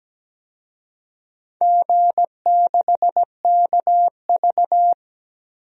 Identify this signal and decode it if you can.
Morse code, heard as a keyed tone, about 17 wpm, 700 Hz. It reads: G6KV
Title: DXCC-100 Call Signs - 4 Characters - 17wpm